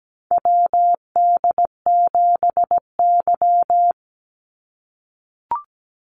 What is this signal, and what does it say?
Morse code, heard as a keyed tone, about 17 wpm, 700 Hz. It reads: WD7Y
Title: NA SOTA Chaser Callsigns 17wpm